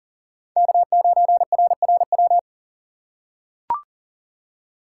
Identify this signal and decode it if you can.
Morse code, heard as a keyed tone, about 40 wpm, 700 Hz. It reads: K9RRW
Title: USA Call Signs - 2x Character Spacing - 40wpm